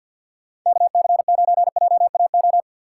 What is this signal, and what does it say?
Morse code, heard as a keyed tone, about 51 wpm, 700 Hz. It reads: KC9JAO
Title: USA Call Signs - 6 Characters - 6x Character Spacing - 50wpm